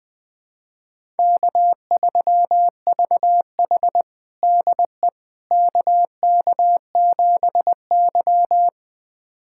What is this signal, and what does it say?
Morse code, heard as a keyed tone, about 20 wpm, 700 Hz. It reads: K3VH DE KK7Y
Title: USA Callsign DE Callsign - 4 Characters - 20wpm